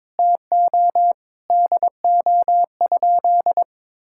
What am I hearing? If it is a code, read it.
Morse code, heard as a keyed tone, about 22 wpm, 700 Hz. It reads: TO DO?
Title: Sentences from Top 1600 Words 22wpm